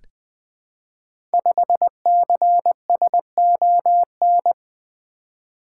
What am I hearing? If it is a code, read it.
Morse code, heard as a keyed tone, about 20 wpm, 700 Hz. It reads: 5CSON
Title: Code Groups - 5 Letter:Numbers - 20wpm